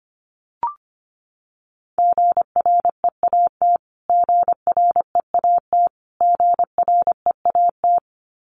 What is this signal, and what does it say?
Morse code, heard as a keyed tone, about 25 wpm, 700 Hz. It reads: GREAT GREAT GREAT
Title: Top 200 Words Repeated 25wpm